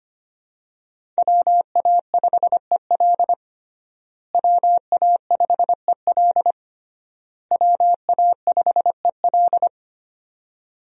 Morse code, heard as a keyed tone, about 25 wpm, 700 Hz. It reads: WA5EL WA5EL WA5EL
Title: USA Call Signs - 5 Characters - Repeated 3x - 25wpm